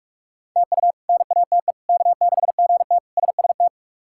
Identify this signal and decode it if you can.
Morse code, heard as a keyed tone, about 41 wpm, 700 Hz. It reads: TU NATE K6GT SST
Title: K1USN SST QSO Practice - 45wpm